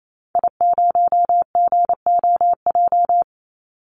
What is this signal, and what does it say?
Morse code, heard as a keyed tone, about 28 wpm, 700 Hz. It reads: I0GOJ
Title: Call Signs Repeated 2x - 28wpm